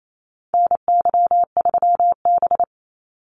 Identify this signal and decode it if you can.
Morse code, heard as a keyed tone, about 28 wpm, 700 Hz. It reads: NY3B
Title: Call Signs - 28wpm